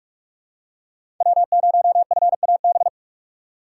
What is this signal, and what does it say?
Morse code, heard as a keyed tone, about 46 wpm, 700 Hz. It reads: W0FAB
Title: USA Call Signs - 6x Character Spacing - 45wpm